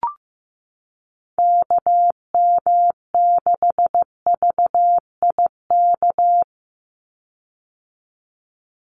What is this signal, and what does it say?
Morse code, heard as a keyed tone, about 15 wpm, 700 Hz. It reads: KM6VIK
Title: USA Call Signs 15wpm